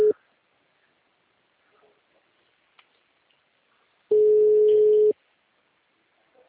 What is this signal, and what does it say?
Ringback tone (Europe, 425 Hz).
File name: out- -2013- - - .49861